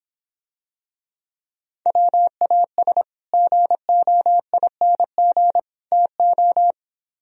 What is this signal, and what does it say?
Morse code, heard as a keyed tone, about 26 wpm, 700 Hz. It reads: WAS GOING TO